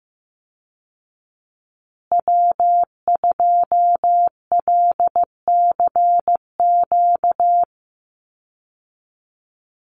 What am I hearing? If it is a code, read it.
Morse code, heard as a keyed tone, about 15 wpm, 700 Hz. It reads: W2LCQ
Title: Call Signs Repeated 3x - 15wpm